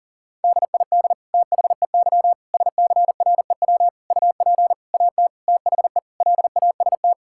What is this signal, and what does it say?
Morse code, heard as a keyed tone, about 40 wpm, 700 Hz. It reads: DID THEY SCREW UP AT THE LAST
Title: Sentences from Top 1600 Words 40wpm